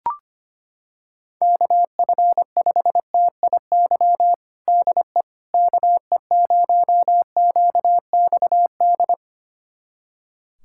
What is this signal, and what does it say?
Morse code, heard as a keyed tone, about 25 wpm, 700 Hz. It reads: KF5TIY DE KE0QXD
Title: USA Callsign DE Callsign - 6 Characters - 25wpm